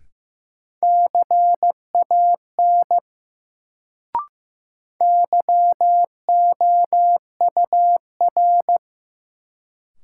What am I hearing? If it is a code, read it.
Morse code, heard as a keyed tone, about 15 wpm, 700 Hz. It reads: CAN YOUR
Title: Sets of 2 Words - New Word Drill - 15wpm